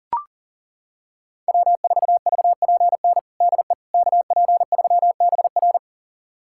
Morse code, heard as a keyed tone, about 40 wpm, 700 Hz. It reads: W4VPN DE KP3BR
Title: USA Callsign DE Callsign - 5 Characters - 40wpm